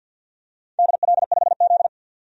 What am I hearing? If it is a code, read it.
Morse code, heard as a keyed tone, about 51 wpm, 700 Hz. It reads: DL5Z
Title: DXCC-100 Call Signs - 4 Characters - 50wpm